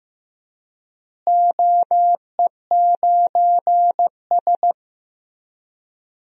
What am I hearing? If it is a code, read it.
Morse code, heard as a keyed tone, about 16 wpm, 700 Hz. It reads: OE9S
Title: DXCC-100 Call Signs - 4 Characters - Repeated 3x - 15wpm